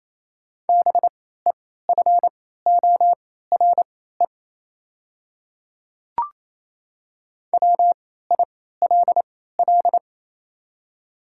Morse code, heard as a keyed tone, about 28 wpm, spaced as for 16 wpm, 700 Hz. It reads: BEFORE WILL
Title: 3x Character Spacing - Top 200 Words - 28wpm